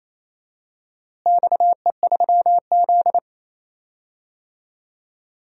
Morse code, heard as a keyed tone, about 28 wpm, 700 Hz. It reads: XE3Z
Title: DXCC-100 Call Signs - 4 Characters - Repeated 2x - 28wpm